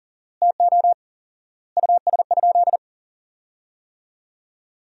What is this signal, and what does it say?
Morse code, heard as a keyed tone, about 40 wpm, 700 Hz. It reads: TO US?
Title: 4x Word-Spacing - Sentences from Top 400 Words - 40wpm